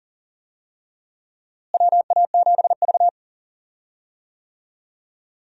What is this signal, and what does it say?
Morse code, heard as a keyed tone, about 40 wpm, 700 Hz. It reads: WA7V
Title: USA Call Signs - 4 Characters - Repeated 2x - 40wpm